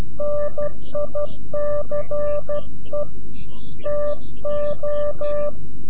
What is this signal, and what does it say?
Morse code, heard as a keyed tone, about 12 wpm, 590 Hz. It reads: NICE TO
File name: CW-QSO / CQ Call / CQ-QSO